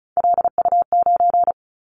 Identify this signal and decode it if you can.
Morse code, heard as a keyed tone, about 36 wpm, 700 Hz. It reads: LU9
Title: Callsign Prefixes for Top 100 DXCC - 35wpm